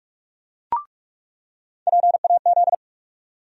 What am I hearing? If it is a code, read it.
Morse code, heard as a keyed tone, about 46 wpm, 700 Hz. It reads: PAZ
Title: USA Call Sign Postfixes - 3 Letters 45wpm